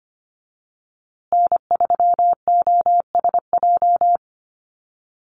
Morse code, heard as a keyed tone, about 25 wpm, 700 Hz. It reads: N3OSJ
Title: USA Call Signs Repeated 2x 25wpm